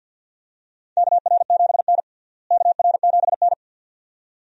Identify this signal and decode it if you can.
Morse code, heard as a keyed tone, about 48 wpm, 690 Hz. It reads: KR7N KR7N
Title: USA Call Signs - 4 Characters - Repeated 2x - 50wpm